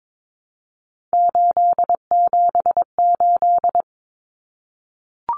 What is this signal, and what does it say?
Morse code, heard as a keyed tone, about 22 wpm, 700 Hz. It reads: 878
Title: Common Cut Numbers - 3-Digit 22wpm